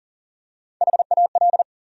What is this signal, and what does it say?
Morse code, heard as a keyed tone, about 40 wpm, 700 Hz. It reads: HAL
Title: USA Common Names - Max 5 Letters - 40wpm